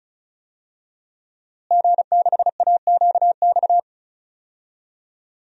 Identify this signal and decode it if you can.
Morse code, heard as a keyed tone, about 36 wpm, 700 Hz. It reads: G6AQX
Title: DXCC-100 Call Signs - 5 Characters - Repeated 3x - 35wpm